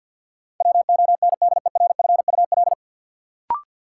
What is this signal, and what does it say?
Morse code, heard as a keyed tone, about 49 wpm, 700 Hz. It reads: WONDERFUL
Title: Sentences from Top 1100 Words - New Word Drill Alt - 50wpm